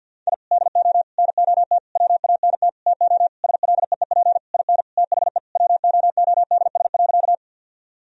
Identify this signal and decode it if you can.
Morse code, heard as a keyed tone, about 50 wpm, 700 Hz. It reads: I DO NOT WANT TO SLEEP IN THE WOODS.
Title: Sentences from Top 1000 Words - Review Part 2 - 50wpm